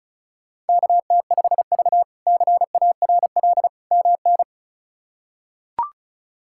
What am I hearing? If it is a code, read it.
Morse code, heard as a keyed tone, about 36 wpm, 700 Hz. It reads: KT5V CARL MN
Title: K1USN SST QSO Practice - 35wpm